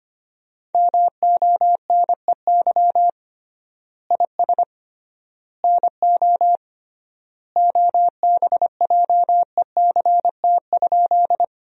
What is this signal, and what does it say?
Morse code, heard as a keyed tone, about 25 wpm, 700 Hz. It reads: MONEY IS NO OBJECT?
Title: 3x Word-Spacing - Sentences from Top 500 Words - 25wpm